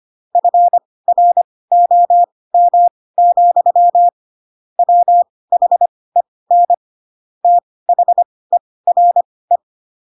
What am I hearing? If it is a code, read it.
Morse code, heard as a keyed tone, about 25 wpm, spaced as for 18 wpm, 700 Hz. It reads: FROM, WHEN THERE
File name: Pinocchio / Pinocchio ch1 12 25 ik1hge